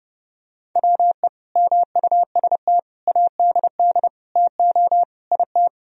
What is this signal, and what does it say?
Morse code, heard as a keyed tone, about 31 wpm, 700 Hz. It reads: WE MUST ADD TO IT
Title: Sentences from Top 1000 Words - Review Part 1 - 30wpm